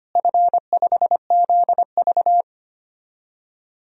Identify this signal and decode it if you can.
Morse code, heard as a keyed tone, about 25 wpm, 700 Hz. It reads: F5ZV
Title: DXCC-100 Call Signs - 4 Characters - Repeated 2x - 25wpm